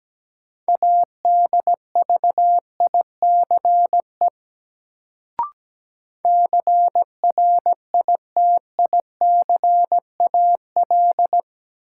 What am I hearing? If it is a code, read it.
Morse code, heard as a keyed tone, about 17 wpm, 700 Hz. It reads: ADVICE CRITICAL
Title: Sentences from Top 1100 Words - New Word Drill Alt - 17wpm